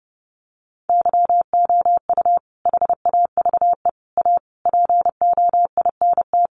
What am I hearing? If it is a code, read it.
Morse code, heard as a keyed tone, about 30 wpm, 700 Hz. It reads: YOU HAVE A POINT
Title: Sentences from Top 1400 Words 30wpm